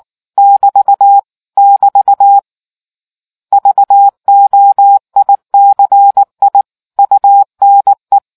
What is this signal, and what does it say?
Morse code, heard as a keyed tone, about 19 wpm, 800 Hz. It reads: <BT> <BT> VOICI UNE